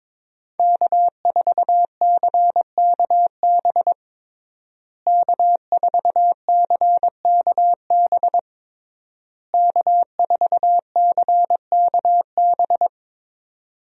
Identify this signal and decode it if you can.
Morse code, heard as a keyed tone, about 22 wpm, 700 Hz. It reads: K4CKB K4CKB K4CKB
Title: USA Call Signs - 5 Characters - Repeated 3x - 22wpm